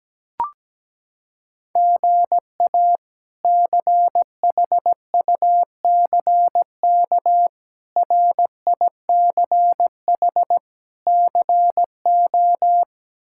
Morse code, heard as a keyed tone, about 17 wpm, 700 Hz. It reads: GA CHUCK RICH CO
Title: K1USN SST QSO Practice - 17wpm